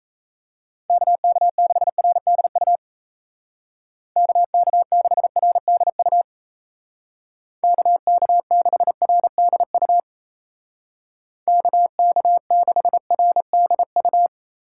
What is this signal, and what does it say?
Morse code, heard as a keyed tone, about 34 wpm, 700 Hz. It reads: KK6RDU KK6RDU KK6RDU KK6RDU
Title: Speed Racing - USA 2x3 Call Signs 28wpm